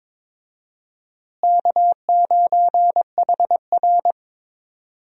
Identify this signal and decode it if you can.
Morse code, heard as a keyed tone, about 22 wpm, 700 Hz. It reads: K9HR
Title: USA Call Signs - 4 Characters - Repeated 3x - 22wpm